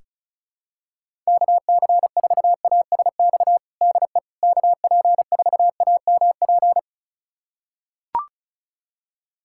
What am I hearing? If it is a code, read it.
Morse code, heard as a keyed tone, about 36 wpm, 700 Hz. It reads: KC4ASX DE KP4AMP
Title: USA Callsign DE Callsign 35wpm